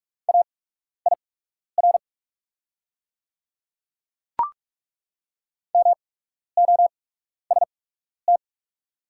Morse code, heard as a keyed tone, about 43 wpm, spaced as for 12 wpm, 700 Hz. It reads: AIR MOST
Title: 8x Character Spacing - Top 300 Words - 45wpm